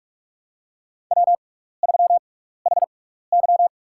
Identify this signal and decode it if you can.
Morse code, heard as a keyed tone, about 47 wpm, spaced as for 16 wpm, 700 Hz. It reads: W3HY
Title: USA Call Signs - 6x Character Spacing - 45wpm